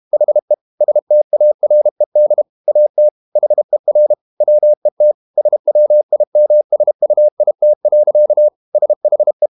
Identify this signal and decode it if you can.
Morse code, heard as a keyed tone, about 32 wpm, 600 Hz. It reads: HE STARED AT HER WET SWIMSUIT. SHE
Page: Never 31wpm | CW med Gnister